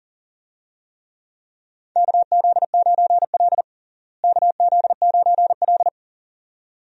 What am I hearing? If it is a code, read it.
Morse code, heard as a keyed tone, about 37 wpm, 700 Hz. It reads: KZ9L KZ9L
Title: USA Call Signs - 4 Characters - Repeated 2x - 40wpm